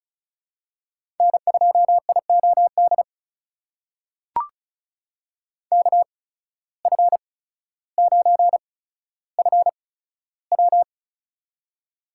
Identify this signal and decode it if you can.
Morse code, heard as a keyed tone, about 34 wpm, spaced as for 15 wpm, 700 Hz. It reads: N2IOD KF9FW
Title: USA Call Signs - 5 Characters - 8x Character Spacing - 35wpm